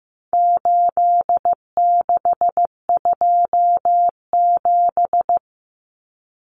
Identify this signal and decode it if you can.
Morse code, heard as a keyed tone, about 15 wpm, 700 Hz. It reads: 8627
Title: Numbers - 4-Digit 15wpm